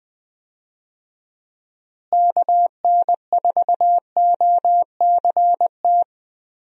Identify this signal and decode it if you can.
Morse code, heard as a keyed tone, about 20 wpm, 700 Hz. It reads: KN4OCT
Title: USA Call Signs - 6 Characters - Repeated 3x - 20wpm